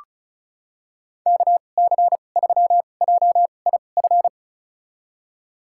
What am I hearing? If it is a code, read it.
Morse code, heard as a keyed tone, about 34 wpm, spaced as for 25 wpm, 700 Hz. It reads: KC3JIF
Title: USA Call Signs - 6 Characters - 2x Character Spacing - 35wpm